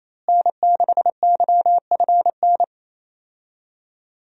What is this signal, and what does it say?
Morse code, heard as a keyed tone, about 28 wpm, 700 Hz. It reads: N6YFN
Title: USA Call Signs - 5 Characters - Repeated 3x - 28wpm